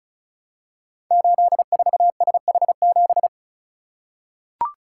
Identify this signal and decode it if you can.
Morse code, heard as a keyed tone, about 36 wpm, 700 Hz. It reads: 84SH7
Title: Code Groups - 5 Letter:Numbers - 35wpm